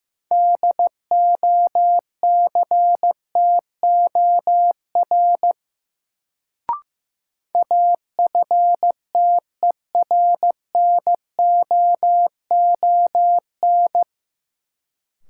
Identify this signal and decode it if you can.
Morse code, heard as a keyed tone, about 15 wpm, 700 Hz. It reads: DOCTOR AFTERNOON
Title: Sentences from Top 1500 Words - New Word Drill - 15wpm